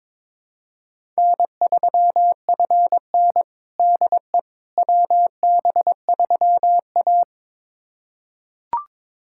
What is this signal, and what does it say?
Morse code, heard as a keyed tone, about 23 wpm, 700 Hz. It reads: N3FN DE WB3A
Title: USA Callsign DE Callsign - 4 Characters - 22wpm